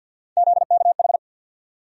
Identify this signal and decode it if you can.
Morse code, heard as a keyed tone, about 48 wpm, 700 Hz. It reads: ZKH
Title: USA Call Sign Postfixes - 3 Letters 50wpm